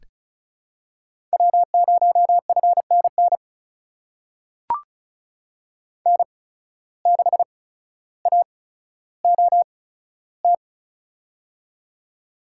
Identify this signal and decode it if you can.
Morse code, heard as a keyed tone, about 36 wpm, spaced as for 15 wpm, 700 Hz. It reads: W0FNN N6AOT
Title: USA Call Signs - 5 Characters - 8x Character Spacing - 35wpm